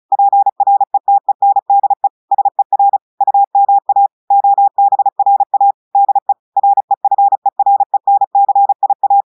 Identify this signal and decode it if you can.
Morse code, heard as a keyed tone, about 36 wpm, 800 Hz. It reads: PRETENDE SER UMA OBRA DE REFERENCIA
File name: audios / portugais / portugais texte en clair 1 vitesse 35 mots minutes